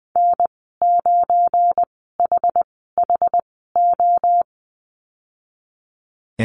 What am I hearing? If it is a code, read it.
Morse code, heard as a keyed tone, about 20 wpm, spaced as for 15 wpm, 700 Hz. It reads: N9HHO
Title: 193 Lesson 31g - Letter v - Call Signs 20wpm